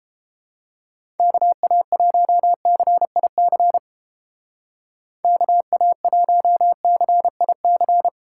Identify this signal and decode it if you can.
Morse code, heard as a keyed tone, about 31 wpm, 700 Hz. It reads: KA1CIC KA1CIC